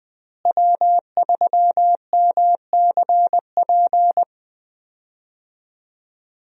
Morse code, heard as a keyed tone, about 20 wpm, 700 Hz. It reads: W3MCP
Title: USA Call Signs - 5 Characters - Repeated 3x - 20wpm